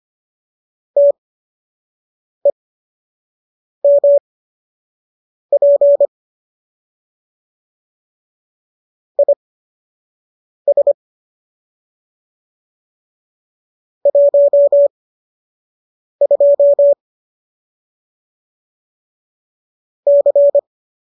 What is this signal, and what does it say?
Morse code, heard as a keyed tone, about 25 wpm, spaced as for 6 wpm, 570 Hz. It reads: TEMP IS 12 C